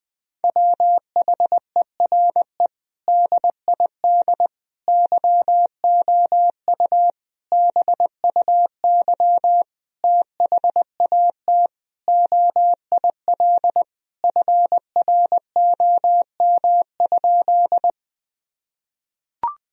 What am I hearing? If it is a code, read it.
Morse code, heard as a keyed tone, about 21 wpm, 700 Hz. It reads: WHERE DID YOU BUY THAT OIL FROM?
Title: Sentences from Top 1000 Words - Review Part 4 - 20wpm